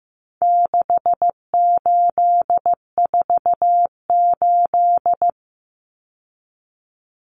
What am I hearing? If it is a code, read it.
Morse code, heard as a keyed tone, about 15 wpm, 700 Hz. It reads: 6848
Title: Numbers - 4-Digit 15wpm